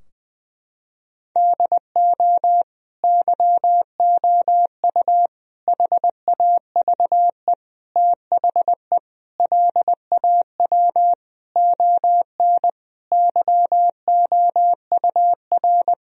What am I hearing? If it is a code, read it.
Morse code, heard as a keyed tone, about 20 wpm, 700 Hz. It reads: DO YOU HAVE THE LAW ON YOUR
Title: Sentences from Top 1000 Words - Review Part 3 - 20wpm